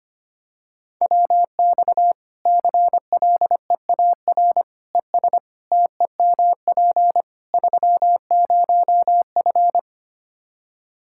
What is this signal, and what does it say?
Morse code, heard as a keyed tone, about 25 wpm, 700 Hz. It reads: WX CLEAR ES TEMP 30F
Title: Practice QSO - Exchange 2 - Info - 25wpm